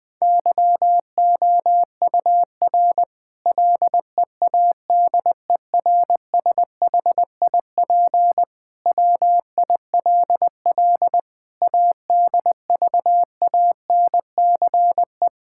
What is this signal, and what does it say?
Morse code, heard as a keyed tone, about 20 wpm, 700 Hz. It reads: YOUR LEADERSHIP WILL ADVANCE
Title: Sentences from Top 1600 Words 20wpm